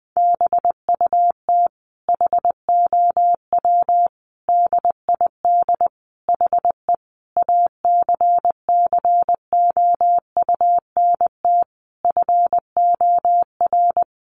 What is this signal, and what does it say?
Morse code, heard as a keyed tone, about 19 wpm, 700 Hz. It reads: BUT HOW DID HE ACCOUNT FOR
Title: Sentences from Top 900 Words 20wpm